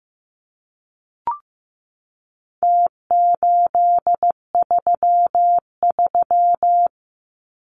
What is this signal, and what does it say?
Morse code, heard as a keyed tone, about 15 wpm, 700 Hz. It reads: T833
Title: Numbers - Contest Style - 4-Digit 15wpm